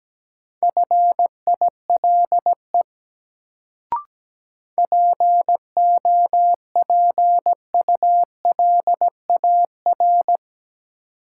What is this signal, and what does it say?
Morse code, heard as a keyed tone, about 17 wpm, 700 Hz. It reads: FILE POPULAR
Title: Sentences from Top 1200 Words - New Word Drill Alt - 17wpm